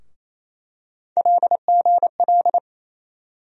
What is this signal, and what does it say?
Morse code, heard as a keyed tone, about 28 wpm, 700 Hz. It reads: LGL
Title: USA Call Sign Postfixes - 3 Letters 28wpm